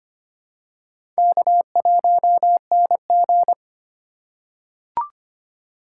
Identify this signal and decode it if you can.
Morse code, heard as a keyed tone, about 25 wpm, 700 Hz. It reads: K1NG
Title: USA Call Signs - 4 Characters - 3x Character Spacing - 25wpm